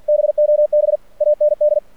Morse code, heard as a keyed tone, about 46 wpm, 600 Hz. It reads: BOB AND
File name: archived / CRSnet